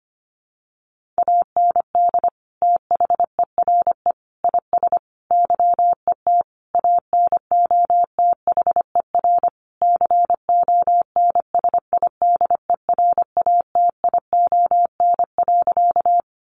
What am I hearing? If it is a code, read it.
Morse code, heard as a keyed tone, about 25 wpm, 700 Hz. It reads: AND THERE IS YET ANOTHER CONSIDERATION.
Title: Sentences from Top 1500 Words 25wpm